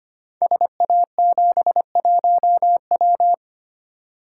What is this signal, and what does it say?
Morse code, heard as a keyed tone, about 25 wpm, 700 Hz. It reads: SA71W
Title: Code Groups - 5 Letter:Numbers - 25wpm